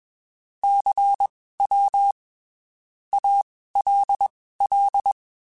Morse code, heard as a keyed tone, about 21 wpm, spaced as for 15 wpm, 800 Hz. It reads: CW ALL
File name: (spaced) / QSOs Spaced-mp3s / WD0FBT @20 sp